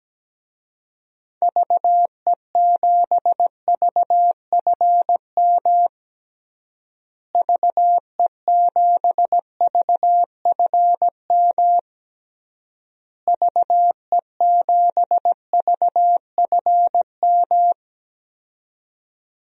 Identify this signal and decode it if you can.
Morse code, heard as a keyed tone, about 17 wpm, 700 Hz. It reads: VE7VFM VE7VFM VE7VFM
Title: DXCC-100 Call Signs - 6 Characters - Repeated 3x - 17wpm